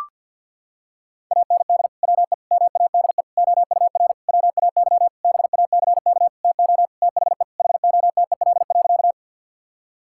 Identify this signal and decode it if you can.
Morse code, heard as a keyed tone, about 50 wpm, 700 Hz. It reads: AND WE MADE OUR WAY BACK TO THE HOTEL.
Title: Sentences from Top 1600 Words 50wpm